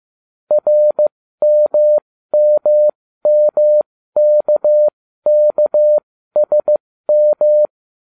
Morse code, heard as a keyed tone, about 15 wpm, spaced as for 13 wpm, 600 Hz. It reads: RMMMKKSM